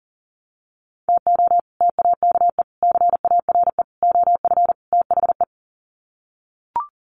Morse code, heard as a keyed tone, about 40 wpm, 700 Hz. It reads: TO TAKE CARE OF THE
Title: Sets of 5 Words Encore 40wpm